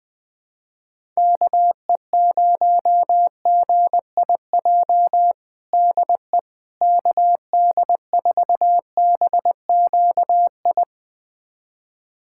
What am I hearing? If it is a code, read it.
Morse code, heard as a keyed tone, about 20 wpm, 700 Hz. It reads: KE0GIJ DE KD4BQI
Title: USA Callsign DE Callsign - 6 Characters - 20wpm